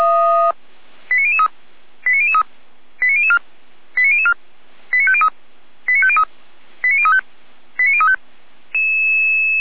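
Five-tone selective call (ZVEI), heard as quick runs of five tones, tones of about 80 ms. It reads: 89003 89003 89004 89004 89503 89503 89035 89035
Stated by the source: ZVEI five-tone selective call 89001 received by SDR, from the Feuerwehr Orlamünde server.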